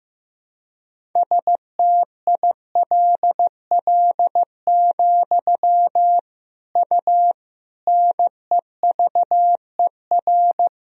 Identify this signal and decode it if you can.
Morse code, heard as a keyed tone, about 15 wpm, 700 Hz. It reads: STILL, U NEVER
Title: Sentences from Top 300 Words and Abbreviations 15wpm